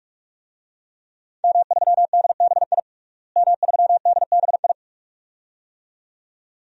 Morse code, heard as a keyed tone, about 46 wpm, 700 Hz. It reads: M3DBI M3DBI
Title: DXCC-100 Call Signs - 5 Characters - Repeated 2x - 45wpm